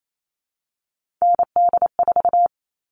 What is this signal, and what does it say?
Morse code, heard as a keyed tone, about 28 wpm, 700 Hz. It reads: ND4
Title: All Cut Numbers - 3-Digit 28wpm